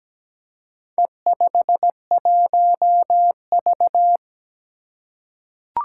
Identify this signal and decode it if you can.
Morse code, heard as a keyed tone, about 17 wpm, 700 Hz. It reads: E51V
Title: Four Challenging Letter-Number Combinations 17wpm